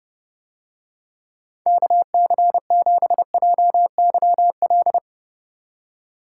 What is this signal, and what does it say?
Morse code, heard as a keyed tone, about 30 wpm, 700 Hz. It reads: KC7JYL